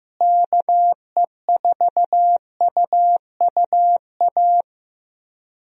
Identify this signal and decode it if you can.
Morse code, heard as a keyed tone, about 15 wpm, 700 Hz. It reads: KE4UUA
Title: USA Call Signs - 6 Characters - Repeated 2x - 15wpm